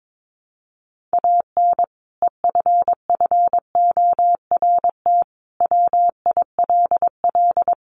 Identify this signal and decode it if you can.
Morse code, heard as a keyed tone, about 22 wpm, 700 Hz. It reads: AN EFFORT WILL
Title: Sentences from Top 1500 Words 22wpm